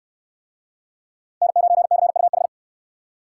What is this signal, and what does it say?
Morse code, heard as a keyed tone, about 69 wpm, 700 Hz. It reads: N9CUH